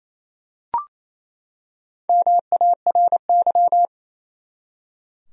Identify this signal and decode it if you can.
Morse code, heard as a keyed tone, about 27 wpm, 700 Hz. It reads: MARY
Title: USA Common Names - Max 4 Letters - 28wpm